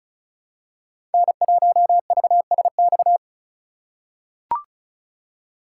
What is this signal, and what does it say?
Morse code, heard as a keyed tone, about 36 wpm, 690 Hz. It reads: N1VSX
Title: USA Call Signs - 5 Characters - 3x Character Spacing - 35wpm